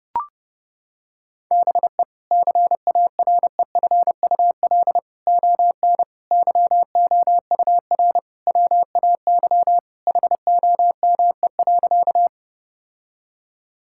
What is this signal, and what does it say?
Morse code, heard as a keyed tone, about 28 wpm, 700 Hz. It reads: BE CAREFUL ON YOUR WAY HOME.
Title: Sentences from Top 1600 Words 30wpm